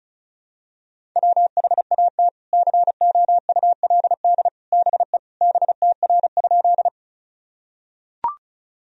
Morse code, heard as a keyed tone, about 36 wpm, 700 Hz. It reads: WHAT COULD BE BTR?
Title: Sentences from Top 400 Words and Abbreviations 35wpm